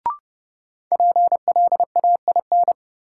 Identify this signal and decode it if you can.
Morse code, heard as a keyed tone, about 31 wpm, 700 Hz. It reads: PLAIN
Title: Sentences from Top 400 Words - New Word Drill - 30wpm